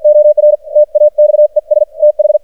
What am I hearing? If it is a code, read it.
Morse code, heard as a keyed tone, about 51 wpm, spaced as for 45 wpm, 600 Hz. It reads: OR TAXES TH
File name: archived / CRSnet